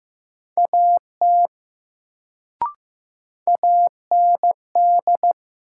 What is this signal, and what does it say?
Morse code, heard as a keyed tone, about 15 wpm, 700 Hz. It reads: AT AND
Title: Sentences from Top 100 Words - New Word Drill - 15wpm